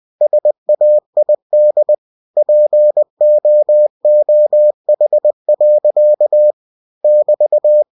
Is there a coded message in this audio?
Yes: Morse code from 0.2 to 7.8 s, heard as a keyed tone, about 20 wpm, 600 Hz. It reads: SAID POOH. <BT>